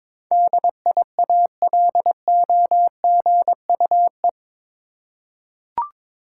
Morse code, heard as a keyed tone, about 21 wpm, 700 Hz. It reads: DIALOGUE
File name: practice / 22wpm / Eight Letter Words - 22wpm